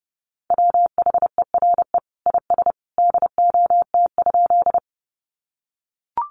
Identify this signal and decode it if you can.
Morse code, heard as a keyed tone, about 30 wpm, 700 Hz. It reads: WHERE IS DOT?
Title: Sentences from Top 1500 Words 30wpm